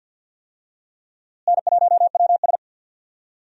Morse code, heard as a keyed tone, about 50 wpm, 700 Hz. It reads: N1WS
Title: USA Call Signs - 4 Characters - Repeated 3x - 50wpm